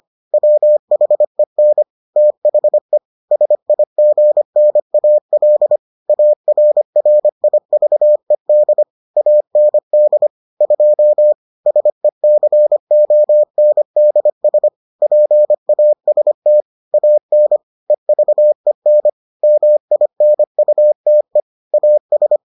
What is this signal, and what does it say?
Morse code, heard as a keyed tone, about 25 wpm, 600 Hz. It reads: WHEN THE SIGNAL ARRIVED AND 2 SECONDS PAST AN EVEN MINUTE AS